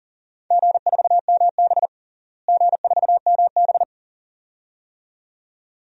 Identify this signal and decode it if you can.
Morse code, heard as a keyed tone, about 40 wpm, 700 Hz. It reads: G4MB G4MB
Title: DXCC-100 Call Signs - 4 Characters - Repeated 2x - 40wpm